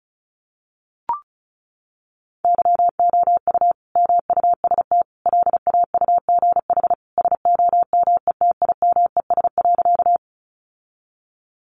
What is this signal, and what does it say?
Morse code, heard as a keyed tone, about 34 wpm, 700 Hz. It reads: YOU MUST LAUGH SOMETIMES.
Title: Sentences from Top 600 Words 35wpm